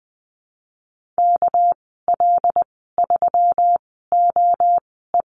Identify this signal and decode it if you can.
Morse code, heard as a keyed tone, about 20 wpm, spaced as for 15 wpm, 700 Hz. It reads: KL3OE
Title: 200 Lesson 32g - Letter k - Call Signs 20wpm